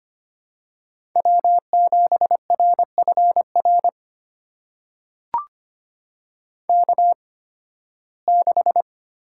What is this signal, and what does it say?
Morse code, heard as a keyed tone, about 25 wpm, spaced as for 16 wpm, 700 Hz. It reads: W7RFR K6
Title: USA Call Signs - 5 Characters - 8x Character Spacing - 25wpm